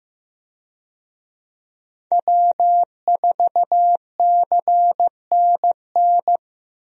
Morse code, heard as a keyed tone, about 15 wpm, 700 Hz. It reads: W4CNN
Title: USA Call Signs - 5 Characters - 15wpm